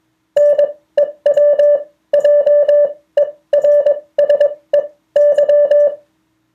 Morse code, heard as a keyed tone, about 21 wpm, spaced as for 16 wpm, 580 Hz. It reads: NEWJERSEY